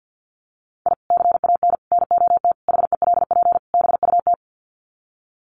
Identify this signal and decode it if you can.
Morse code, heard as a keyed tone, about 50 wpm, 700 Hz. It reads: I CAN NOT HELP BUT
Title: Sets of 5 Words Encore2 50wpm